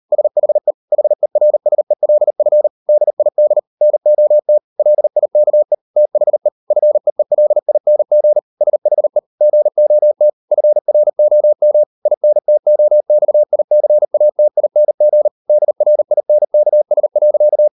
Morse code, heard as a keyed tone, about 38 wpm, 600 Hz. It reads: SHE HERSELF DID NOT LIKE THE FEELING SHE GOT FROM INTOXICATING DRINKS.